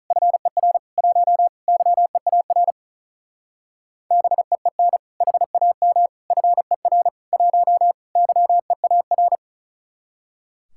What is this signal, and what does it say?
Morse code, heard as a keyed tone, about 42 wpm, 700 Hz. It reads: FER 1 YEAR BEEN HAM FER 1 YEAR
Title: Speed-Racing Common QSO Phrases - 35wpm